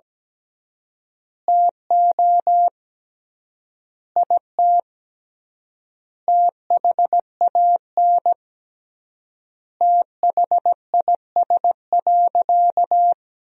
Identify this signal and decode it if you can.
Morse code, heard as a keyed tone, about 17 wpm, 700 Hz. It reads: TO IT THAN THIS.
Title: 3x Word-Spacing - Sentences from Top 100 Words - 17wpm